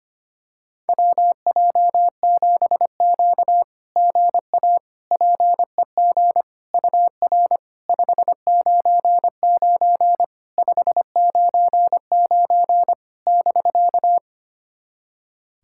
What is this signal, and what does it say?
Morse code, heard as a keyed tone, about 25 wpm, 700 Hz. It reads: WJ7Q GA PEG UR 599 599 <BK>
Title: SOTA QSO Practice 25wpm